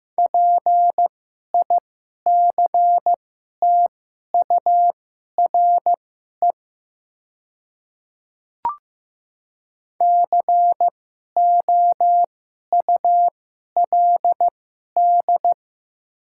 Morse code, heard as a keyed tone, about 15 wpm, spaced as for 11 wpm, 700 Hz. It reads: PICTURE COULD
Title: 2x Character Spacing - Top 200 Words - 15wpm